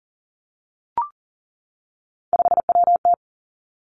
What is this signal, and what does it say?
Morse code, heard as a keyed tone, about 40 wpm, 700 Hz. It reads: 5WT
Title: Three Challenging Letter-Number Combinations 40wpm